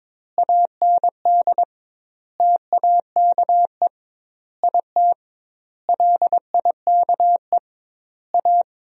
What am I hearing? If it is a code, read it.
Morse code, heard as a keyed tone, about 22 wpm, 700 Hz. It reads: AND TAKE IT LIKE A
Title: 2x Word-Spacing - Sentences from Top 300 Words - 22wpm